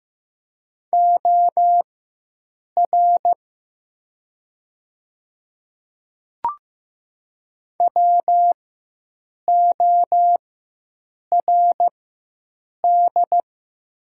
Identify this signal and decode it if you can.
Morse code, heard as a keyed tone, about 15 wpm, spaced as for 7 wpm, 700 Hz. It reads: OR WORD
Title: 4x Character Spacing - Top 300 Words - 15wpm